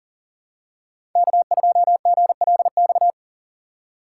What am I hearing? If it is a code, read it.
Morse code, heard as a keyed tone, about 37 wpm, 700 Hz. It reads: K2GLX
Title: USA Call Signs - 5 Characters - 6x Character Spacing - 40wpm